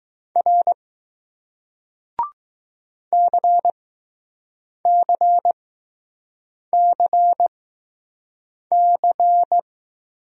Morse code, heard as a keyed tone, about 20 wpm, spaced as for 6 wpm, 700 Hz. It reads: R CCCC